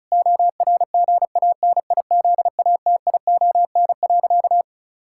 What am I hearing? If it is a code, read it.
Morse code, heard as a keyed tone, about 36 wpm, 700 Hz. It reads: ORGANIZATION.
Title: Sentences from Top 1000 Words - Review Part 5 - 35wpm